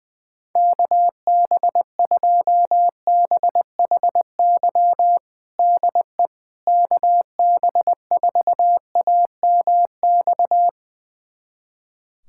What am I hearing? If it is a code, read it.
Morse code, heard as a keyed tone, about 20 wpm, 700 Hz. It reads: KB2BHY DE KB4AMX
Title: USA Callsign DE Callsign - 6 Characters - 20wpm